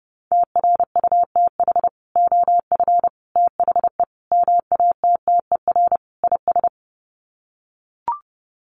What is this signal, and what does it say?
Morse code, heard as a keyed tone, about 30 wpm, 700 Hz. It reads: TRUTH OF THE MATTER IS
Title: Sets of 5 Words Encore2 30wpm